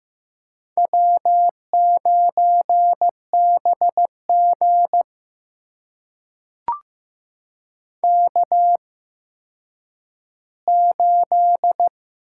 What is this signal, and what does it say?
Morse code, heard as a keyed tone, about 15 wpm, spaced as for 9 wpm, 700 Hz. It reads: W9BG K8
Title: USA Call Signs - 4 Characters - 8x Character Spacing - 15wpm